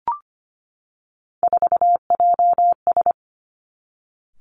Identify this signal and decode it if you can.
Morse code, heard as a keyed tone, about 25 wpm, 700 Hz. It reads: 4JS
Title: Three Challenging Letter-Number Combinations 25wpm